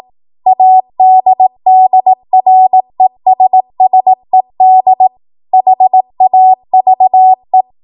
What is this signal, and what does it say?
Morse code, heard as a keyed tone, about 18 wpm, 750 Hz. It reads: ADDRESSED HAVE
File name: Morse / 18WPM